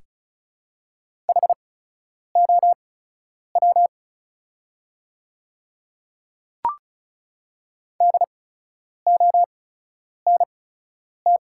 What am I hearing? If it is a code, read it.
Morse code, heard as a keyed tone, about 36 wpm, spaced as for 10 wpm, 700 Hz. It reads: HOW DONT
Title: 8x Character Spacing - Top 300 Words - 35wpm